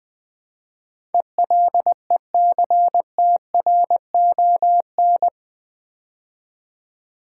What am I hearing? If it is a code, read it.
Morse code, heard as a keyed tone, about 20 wpm, 700 Hz. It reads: ELECTRON
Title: Build A Word - 4 Letters At A Time - 10-Letter Words - 20wpm